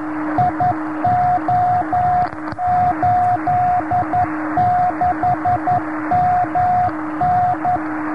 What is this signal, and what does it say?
Morse code, heard as a keyed tone, about 11 wpm, 720 Hz, with noise. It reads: IO86MN
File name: gb3ang in jo70fd